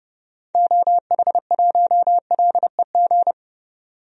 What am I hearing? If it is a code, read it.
Morse code, heard as a keyed tone, about 30 wpm, 700 Hz. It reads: OH1LEG
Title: DXCC-100 Call Signs - 6 Characters - 30wpm